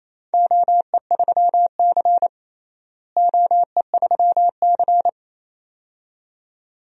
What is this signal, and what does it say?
Morse code, heard as a keyed tone, about 28 wpm, 700 Hz. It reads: OE3C OE3C
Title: DXCC-100 Call Signs - 4 Characters - Repeated 2x - 28wpm